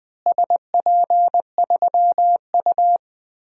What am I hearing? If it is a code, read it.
Morse code, heard as a keyed tone, about 20 wpm, 700 Hz. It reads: SP3U
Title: DXCC-100 Call Signs - 4 Characters - Repeated 2x - 20wpm